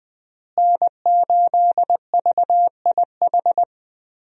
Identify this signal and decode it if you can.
Morse code, heard as a keyed tone, about 20 wpm, 700 Hz. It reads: N8VIH
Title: USA Call Signs - 5 Characters - Repeated 2x - 20wpm